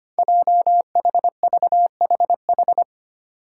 Morse code, heard as a keyed tone, about 25 wpm, 700 Hz. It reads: JHVHH
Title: Five Challenging Letter-Number Combinations 25wpm